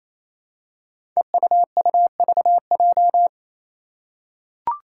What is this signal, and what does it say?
Morse code, heard as a keyed tone, about 27 wpm, 700 Hz. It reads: EUUVJ
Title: Five Challenging Letter-Number Combinations 28wpm